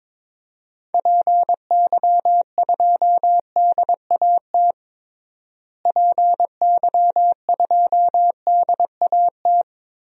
Morse code, heard as a keyed tone, about 22 wpm, 700 Hz. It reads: PY2DAT PY2DAT
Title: DXCC-100 Call Signs - 6 Characters - Repeated 3x - 22wpm